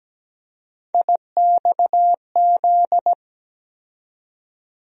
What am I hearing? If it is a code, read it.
Morse code, heard as a keyed tone, about 17 wpm, 700 Hz. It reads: IXZ
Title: USA Call Sign Postfixes - 3 Letters 17wpm